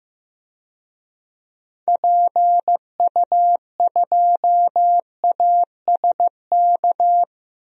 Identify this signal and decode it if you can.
Morse code, heard as a keyed tone, about 15 wpm, 700 Hz. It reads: PU2ASK
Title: DXCC-100 Call Signs - 6 Characters - Repeated 3x - 15wpm